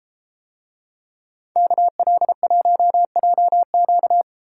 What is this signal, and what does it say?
Morse code, heard as a keyed tone, about 32 wpm, 700 Hz. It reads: KL1JQ